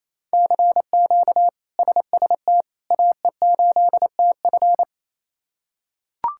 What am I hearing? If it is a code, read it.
Morse code, heard as a keyed tone, about 28 wpm, 700 Hz. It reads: CQ SST AE8TF
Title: K1USN SST QSO Practice - 28wpm